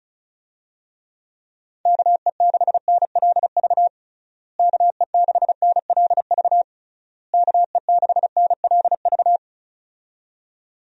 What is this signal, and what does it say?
Morse code, heard as a keyed tone, about 34 wpm, 700 Hz. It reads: KE6NLV KE6NLV KE6NLV
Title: USA Call Signs - 6 Characters - Repeated 3x - 35wpm